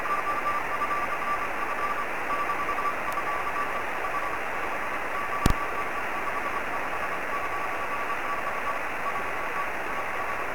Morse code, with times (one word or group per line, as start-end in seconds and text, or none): none